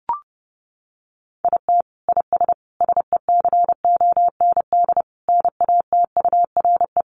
Morse code, heard as a keyed tone, about 28 wpm, 700 Hz. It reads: IT IS SECOND NATURE
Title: Sentences from Top 600 Words 30wpm